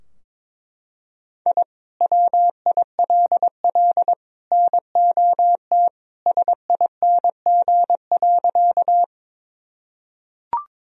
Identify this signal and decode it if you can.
Morse code, heard as a keyed tone, about 22 wpm, 700 Hz. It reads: I WILL NOT SING.
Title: Sentences from Top 1000 Words - Review Part 2 - 22wpm